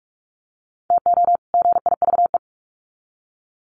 Morse code, heard as a keyed tone, about 44 wpm, 700 Hz. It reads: TO GIVE
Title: Sets of 2 Words 45wpm